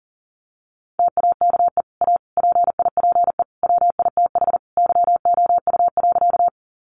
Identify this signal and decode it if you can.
Morse code, heard as a keyed tone, about 40 wpm, 700 Hz. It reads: TAKE A PIPE WITH YOU.
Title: Sentences from Top 1500 Words 40wpm